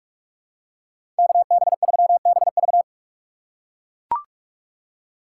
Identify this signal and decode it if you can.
Morse code, heard as a keyed tone, about 42 wpm, 700 Hz. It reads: KB3BV
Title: USA Call Signs - 6x Character Spacing - 45wpm